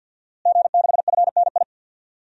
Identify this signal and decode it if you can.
Morse code, heard as a keyed tone, about 52 wpm, 700 Hz. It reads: G6FNI
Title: DXCC-100 Call Signs - 5 Characters - Repeated 3x - 50wpm